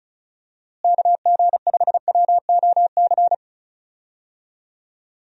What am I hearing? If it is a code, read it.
Morse code, heard as a keyed tone, about 36 wpm, 700 Hz. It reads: KG5WOC
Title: USA Call Signs - 6 Characters - Repeated 3x - 35wpm